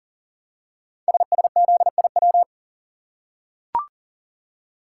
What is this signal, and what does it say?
Morse code, heard as a keyed tone, about 40 wpm, 700 Hz. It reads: SSZIW
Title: Code Groups - 5 Letters - 40wpm